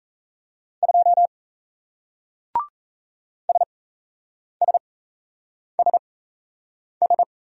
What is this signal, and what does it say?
Morse code, heard as a keyed tone, about 37 wpm, spaced as for 8 wpm, 700 Hz. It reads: 2 SSSS